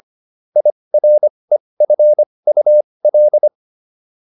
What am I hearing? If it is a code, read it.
Morse code, heard as a keyed tone, about 25 wpm, spaced as for 20 wpm, 600 Hz. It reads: IREFUL